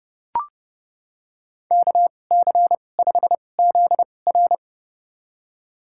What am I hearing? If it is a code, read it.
Morse code, heard as a keyed tone, about 30 wpm, spaced as for 22 wpm, 700 Hz. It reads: KC5ZR
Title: USA Call Signs - 2x Character Spacing - 30wpm